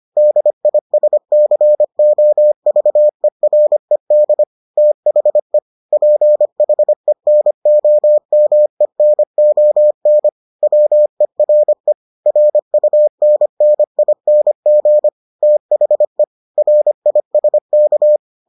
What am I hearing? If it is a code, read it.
Morse code, heard as a keyed tone, about 25 wpm, 600 Hz. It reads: DISCOVERED THE PHENOMENON WERE RUNNING THE RISK